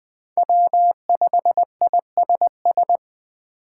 Morse code, heard as a keyed tone, about 20 wpm, 700 Hz. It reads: W5ISS
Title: USA Call Signs - 5 Characters - 8x Character Spacing - 20wpm